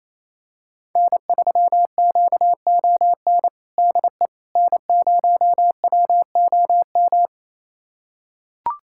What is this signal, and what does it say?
Morse code, heard as a keyed tone, about 28 wpm, 700 Hz. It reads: N3QON DE N0WOM
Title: USA Callsign DE Callsign - 5 Characters - 28wpm